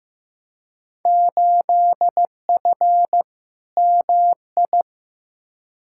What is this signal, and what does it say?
Morse code, heard as a keyed tone, about 15 wpm, 700 Hz. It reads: 8F MI
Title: Field Day Exchanges Encore - 15wpm